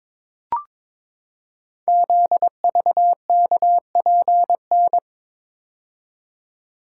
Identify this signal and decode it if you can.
Morse code, heard as a keyed tone, about 21 wpm, 700 Hz. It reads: ZVKPN
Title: Code Groups - 5 Letters - 22wpm